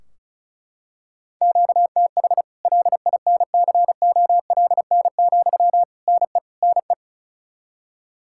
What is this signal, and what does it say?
Morse code, heard as a keyed tone, about 36 wpm, 700 Hz. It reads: QTH LINCOLN, NE NE
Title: Practice QSO - Exchange 1 - Beginning - 35wpm